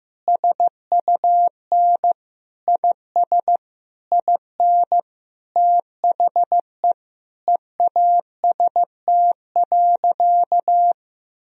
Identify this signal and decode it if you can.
Morse code, heard as a keyed tone, about 15 wpm, 700 Hz. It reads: SUN IS IN THE EAST.
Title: Sentences from Top 1000 Words - Review Part 3 - 15wpm